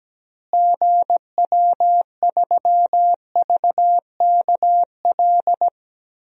Morse code, heard as a keyed tone, about 17 wpm, 700 Hz. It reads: GW3VKL
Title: DXCC-100 Call Signs - 6 Characters - Repeated 3x - 17wpm